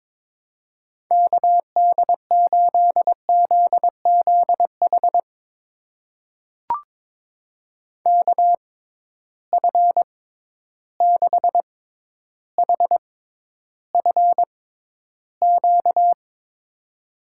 Morse code, heard as a keyed tone, about 22 wpm, spaced as for 11 wpm, 700 Hz. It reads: KD8ZZH KF6HFQ
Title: USA Call Signs - 6 Characters - 6x Character Spacing - 22wpm